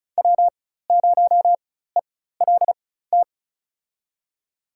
Morse code, heard as a keyed tone, about 36 wpm, spaced as for 16 wpm, 700 Hz. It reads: W0ELT
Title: USA Call Signs - 5 Characters - 4x Character Spacing - 35wpm